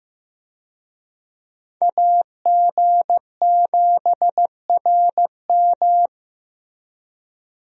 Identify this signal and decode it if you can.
Morse code, heard as a keyed tone, about 15 wpm, 700 Hz. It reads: AG7RM
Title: USA Call Signs - 5 Characters - Repeated 3x - 15wpm